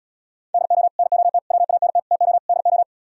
Morse code, heard as a keyed tone, about 77 wpm, 700 Hz. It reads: DO NOT WANT TO DO